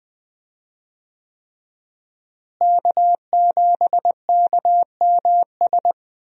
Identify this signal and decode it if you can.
Morse code, heard as a keyed tone, about 21 wpm, 700 Hz. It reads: K7KMS
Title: USA Call Signs - 5 Characters - Repeated 3x - 20wpm